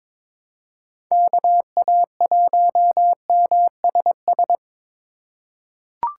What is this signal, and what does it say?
Morse code, heard as a keyed tone, about 22 wpm, 700 Hz. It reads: KA1MSS
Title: USA Call Signs - 6 Characters - 4x Character Spacing - 22wpm